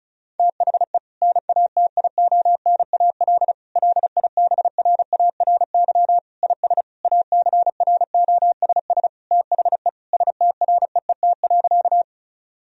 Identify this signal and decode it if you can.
Morse code, heard as a keyed tone, about 36 wpm, 700 Hz. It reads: THE NATIONAL LIBRARY IS ACROSS THE STREET.
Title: Sentences from Top 1000 Words - Review Part 4 - 35wpm